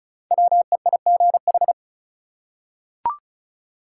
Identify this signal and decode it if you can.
Morse code, heard as a keyed tone, about 36 wpm, 700 Hz. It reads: WEIGH
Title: Five Letter Words - 35wpm